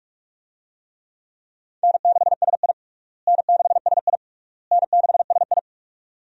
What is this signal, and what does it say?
Morse code, heard as a keyed tone, about 41 wpm, 700 Hz. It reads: N6SI N6SI N6SI
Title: USA Call Signs - 4 Characters - Repeated 3x - 45wpm